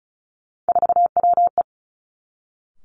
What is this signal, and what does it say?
Morse code, heard as a keyed tone, about 36 wpm, 700 Hz. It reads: 4WE
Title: Three Challenging Letter-Number Combinations 35wpm